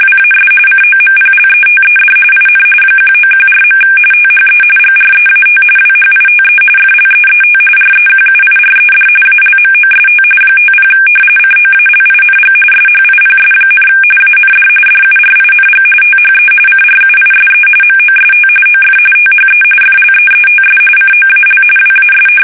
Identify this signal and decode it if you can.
RTTY, about 45.45 baud. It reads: WAS COMPLETED IN 1914. THE GREAT WHITE FLEET REQUIRED ALMOST 50 COALING SHIPS, AND DURING THE CRUISE MOST OF THE FLEETS COAL WAS
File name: fsk850 / usnavy3